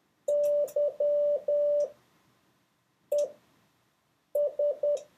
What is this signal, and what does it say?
Morse code, heard as a keyed tone, about 10 wpm, spaced as for 5 wpm, 580 Hz. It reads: YES
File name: 4of6-words-1